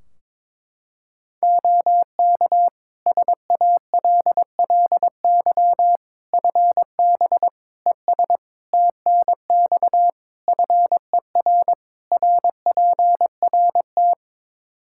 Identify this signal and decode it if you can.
Morse code, heard as a keyed tone, about 22 wpm, 700 Hz. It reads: OK SALLY FB ES TNX FER RPRT
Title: Practice QSO - Exchange 2 - Info - 22wpm